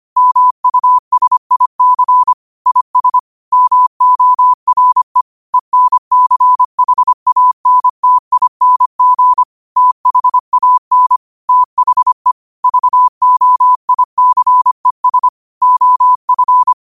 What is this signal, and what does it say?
Morse code, heard as a keyed tone, about 25 wpm, 1000 Hz. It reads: MUSIC IS MORE ENCHANTING THAN THE VOICES OF